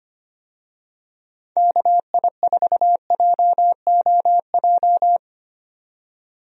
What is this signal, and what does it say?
Morse code, heard as a keyed tone, about 25 wpm, 700 Hz. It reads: KI4JOJ
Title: USA Call Signs - 6 Characters - 25wpm